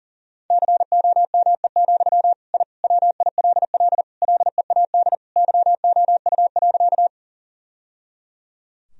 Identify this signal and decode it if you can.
Morse code, heard as a keyed tone, about 40 wpm, 700 Hz. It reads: COME, I WILL LEAD YOU.
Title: Sentences from Top 1000 Words - Review Part 2 - 40wpm